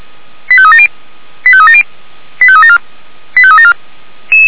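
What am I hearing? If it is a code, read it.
Five-tone selective call (ZVEI), heard as quick runs of five tones, tones of about 70 ms. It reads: 85370 85370 85374 85374
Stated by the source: ZVEI five-tone selective call received via SDR.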